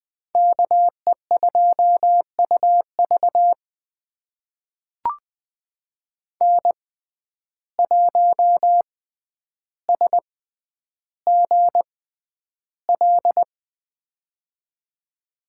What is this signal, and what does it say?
Morse code, heard as a keyed tone, about 20 wpm, spaced as for 10 wpm, 700 Hz. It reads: KE2UV N1SGL
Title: USA Call Signs - 5 Characters - 6x Character Spacing - 20wpm